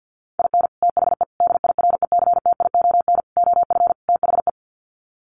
Morse code, heard as a keyed tone, about 50 wpm, 700 Hz. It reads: IN THE DIRECTION OF THE
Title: Sets of 5 Words 50wpm